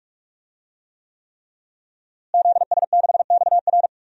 Morse code, heard as a keyed tone, about 46 wpm, 700 Hz. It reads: ZS6XR
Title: DXCC-100 Call Signs - 5 Characters - Repeated 3x - 45wpm